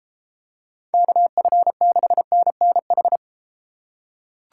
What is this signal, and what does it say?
Morse code, heard as a keyed tone, about 31 wpm, 700 Hz. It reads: KF6NNH